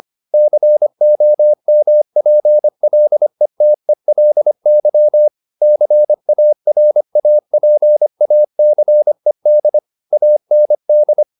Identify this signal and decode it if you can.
Morse code, heard as a keyed tone, about 25 wpm, 600 Hz. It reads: COMPLETELY CARAPACED AND
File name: the old man and the sea part 1 25wpm